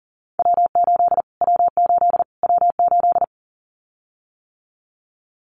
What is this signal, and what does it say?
Morse code, heard as a keyed tone, about 40 wpm, 700 Hz. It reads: W8 W8 W8
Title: Callsign Prefixes for Top 25 DXCC - Repeated 3x - 40wpm